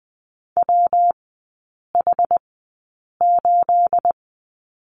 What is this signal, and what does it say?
Morse code, heard as a keyed tone, about 20 wpm, spaced as for 8 wpm, 690 Hz. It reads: WH8
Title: 212 Lesson 34e - Number 8 - Three Characters 20wpm